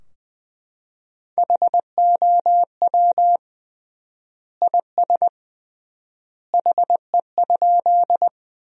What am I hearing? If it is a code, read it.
Morse code, heard as a keyed tone, about 20 wpm, 700 Hz. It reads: HOW IS HE?
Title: 3x Word-Spacing - Sentences from Top 100 Words - 20wpm